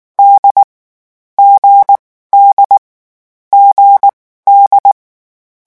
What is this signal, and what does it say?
Morse code, heard as a keyed tone, about 19 wpm, spaced as for 14 wpm, 800 Hz. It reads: D G D GD
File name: morse mp3 / lecon4-vitesse 18